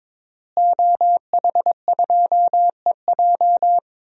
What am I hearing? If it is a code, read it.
Morse code, heard as a keyed tone, about 22 wpm, 700 Hz. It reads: OH2EJ
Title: DXCC-100 Call Signs - 5 Characters - Repeated 3x - 22wpm